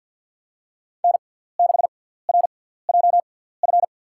Morse code, heard as a keyed tone, about 53 wpm, spaced as for 17 wpm, 700 Hz. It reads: N6RJF
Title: USA Call Signs - 6x Character Spacing - 50wpm